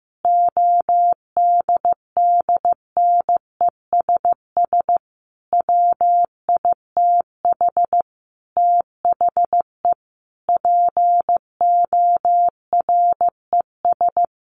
Morse code, heard as a keyed tone, about 15 wpm, 700 Hz. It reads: ODDNESS WITH THE PORES
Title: A Sound Of Thunder 15wpm